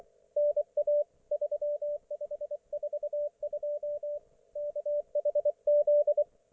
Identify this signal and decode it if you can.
Morse code, heard as a keyed tone, about 24 wpm, 590 Hz, with radio noise. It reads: NA 3542 KHZ
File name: Archiv / CUC212 / ok1fhi cul212